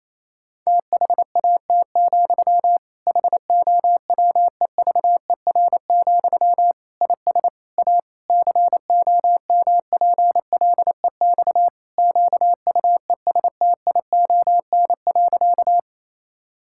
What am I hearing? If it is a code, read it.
Morse code, heard as a keyed tone, about 28 wpm, 700 Hz. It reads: THAT, HOWEVER, IS A COMPLEX QUESTION.
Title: Sentences from Top 1000 Words - Review Part 6 - 28wpm